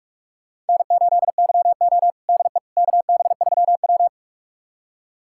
Morse code, heard as a keyed tone, about 42 wpm, 700 Hz. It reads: N8YO DE KB3W
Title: USA Callsign DE Callsign - 4 Characters - 45wpm